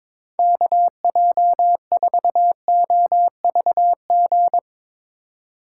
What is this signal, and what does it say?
Morse code, heard as a keyed tone, about 23 wpm, 700 Hz. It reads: KJ4OVG
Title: USA Call Signs - 6 Characters - 4x Character Spacing - 22wpm